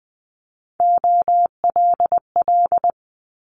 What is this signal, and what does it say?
Morse code, heard as a keyed tone, about 20 wpm, 700 Hz. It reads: OLL
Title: Multi-letter Phonemes Quick Fire 20wpm